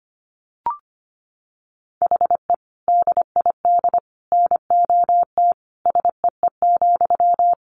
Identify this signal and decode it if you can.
Morse code, heard as a keyed tone, about 25 wpm, 700 Hz. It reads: HE DID NOT SEE,
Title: Sentences from Top 1300 Words 25wpm